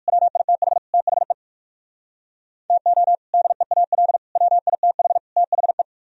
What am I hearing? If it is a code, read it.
Morse code, heard as a keyed tone, about 57 wpm, 700 Hz. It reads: WITH THE TO DEAL WITH THE